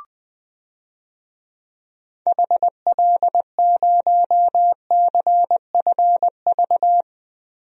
Morse code, heard as a keyed tone, about 20 wpm, 700 Hz. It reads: HL0CFV
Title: DXCC-100 Call Signs - 6 Characters - Repeated 2x - 20wpm